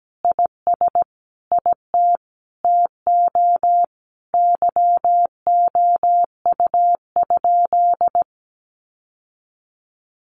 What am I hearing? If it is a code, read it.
Morse code, heard as a keyed tone, about 17 wpm, 700 Hz. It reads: IS IT TO YOU?
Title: Sentences from Top 800 Words 17wpm